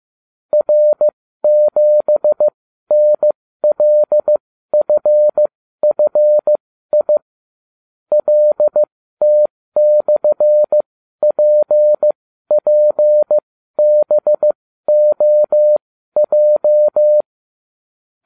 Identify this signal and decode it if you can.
Morse code, heard as a keyed tone, about 15 wpm, spaced as for 12 wpm, 600 Hz. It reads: R7NLFFI LT/PPBOJ